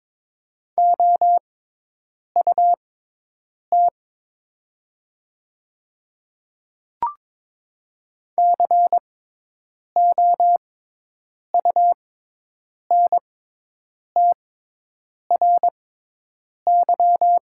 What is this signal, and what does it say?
Morse code, heard as a keyed tone, about 22 wpm, spaced as for 8 wpm, 700 Hz. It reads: OUT COUNTRY
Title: 6x Character Spacing - Top 300 Words - 22wpm